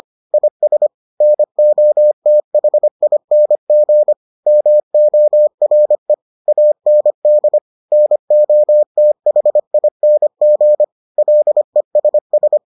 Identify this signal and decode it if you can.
Morse code, heard as a keyed tone, about 25 wpm, 600 Hz. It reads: IS NOTHING MORE AND NOTHING LESS